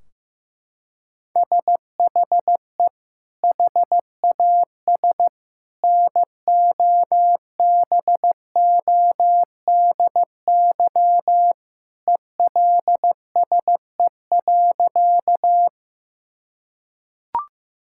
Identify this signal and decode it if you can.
Morse code, heard as a keyed tone, about 15 wpm, 700 Hz. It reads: SHE HAS NOBODY ELSE.
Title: Sentences from Top 1600 Words 15wpm